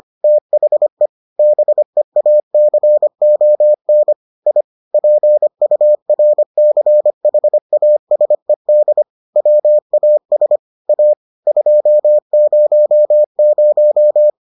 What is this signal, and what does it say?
Morse code, heard as a keyed tone, about 25 wpm, 600 Hz. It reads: THE BEACON I PURCHASED WAS A 200